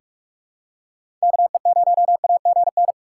Morse code, heard as a keyed tone, about 46 wpm, 700 Hz. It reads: KE0AGN
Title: USA Call Signs - 6 Characters - 2x Character Spacing - 45wpm